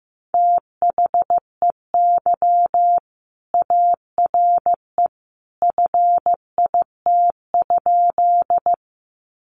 Morse code, heard as a keyed tone, about 15 wpm, 700 Hz. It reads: THEY ARE FIT?
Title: Sentences from Top 800 Words 15wpm